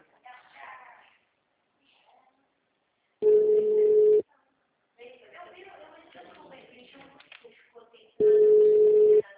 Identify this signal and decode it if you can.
Ringback tone (Europe, 425 Hz), with speech.
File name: out- -2005- - - .41464